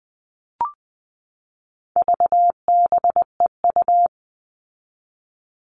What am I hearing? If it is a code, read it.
Morse code, heard as a keyed tone, about 20 wpm, 700 Hz. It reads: VBEU
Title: All Cut Numbers - 4-Digit 20wpm